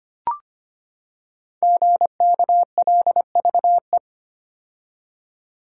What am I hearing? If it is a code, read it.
Morse code, heard as a keyed tone, about 25 wpm, 700 Hz. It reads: GKLVE
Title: Code Groups - 5 Letter:Numbers - 25wpm